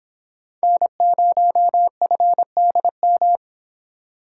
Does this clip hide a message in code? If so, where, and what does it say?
Yes: Morse code from 0.6 to 3.4 s, heard as a keyed tone, about 26 wpm, 700 Hz. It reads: N0FDM